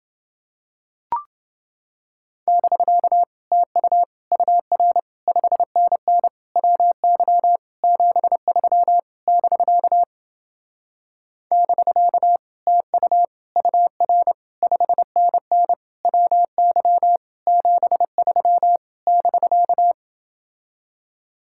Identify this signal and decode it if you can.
Morse code, heard as a keyed tone, about 28 wpm, 700 Hz. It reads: <BK> TU UR 5NN WY 73 <BK> <BK> TU UR 5NN WY 73 <BK>